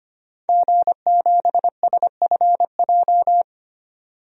Morse code, heard as a keyed tone, about 25 wpm, 700 Hz. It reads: G7SFJ
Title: DXCC-100 Call Signs - 5 Characters - 25wpm